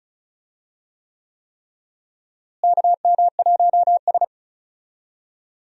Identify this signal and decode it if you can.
Morse code, heard as a keyed tone, about 36 wpm, 700 Hz. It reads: KM1S
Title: USA Call Signs - 4 Characters - Repeated 3x - 35wpm